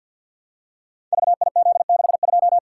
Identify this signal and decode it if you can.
Morse code, heard as a keyed tone, about 50 wpm, 700 Hz. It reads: VIZ62
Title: Code Groups - 5 Letter:Numbers - 50wpm